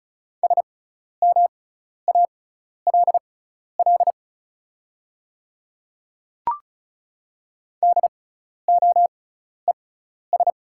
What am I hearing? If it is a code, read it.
Morse code, heard as a keyed tone, about 34 wpm, spaced as for 12 wpm, 690 Hz. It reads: SMALL DOES
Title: 6x Character Spacing - Top 200 Words - 35wpm